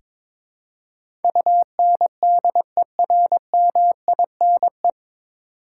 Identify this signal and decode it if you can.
Morse code, heard as a keyed tone, about 22 wpm, 700 Hz. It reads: UNDERMINE
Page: Morse Code Ninja practice 22wpm